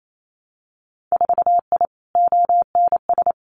Morse code, heard as a keyed tone, about 28 wpm, 700 Hz. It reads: 4I ONS
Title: Winter Field Day Exchanges Encore 28wpm